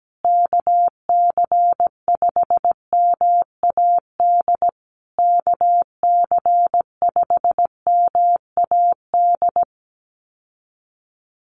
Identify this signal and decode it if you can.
Morse code, heard as a keyed tone, about 17 wpm, 700 Hz. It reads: KC5MAD KC5MAD
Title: USA Call Signs Repeated 3x 17wpm